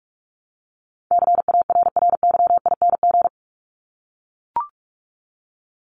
Morse code, heard as a keyed tone, about 44 wpm, 700 Hz. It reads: CARRYING
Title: ING Suffix - Top 100 Words with Base Words 45wpm